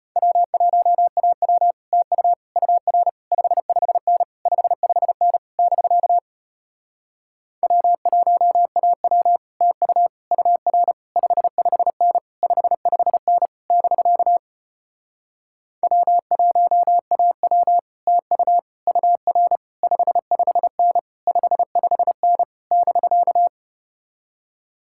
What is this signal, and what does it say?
Morse code, heard as a keyed tone, about 33 wpm, 700 Hz. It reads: W1AW TU UR 55N 55N <BK> W1AW TU UR 55N 55N <BK> W1AW TU UR 55N 55N <BK>